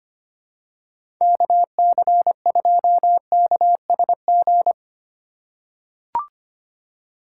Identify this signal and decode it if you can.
Morse code, heard as a keyed tone, about 25 wpm, 700 Hz. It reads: KC2KSG
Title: USA Call Signs - 6 Characters - 3x Character Spacing - 25wpm